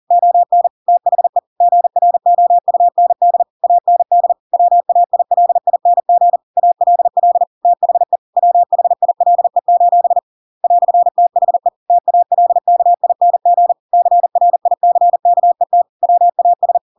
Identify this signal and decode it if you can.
Morse code, heard as a keyed tone, about 40 wpm, 700 Hz. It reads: ON THE GROUND AND WAILING ALL THE WHILE: "THE TALKING CRICKET WAS
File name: Pinocchio / Pinocchio ch5 40 40 ik1hge